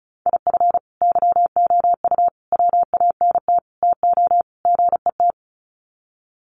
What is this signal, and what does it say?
Morse code, heard as a keyed tone, about 34 wpm, 700 Hz. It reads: IF YOU WANT TO GET
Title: Sets of 5 Words Encore 35wpm